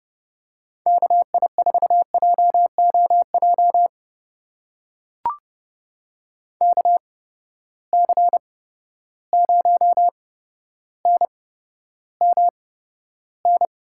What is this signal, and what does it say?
Morse code, heard as a keyed tone, about 29 wpm, spaced as for 13 wpm, 700 Hz. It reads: KI4JOJ KC0NMN
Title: USA Call Signs - 6 Characters - 8x Character Spacing - 30wpm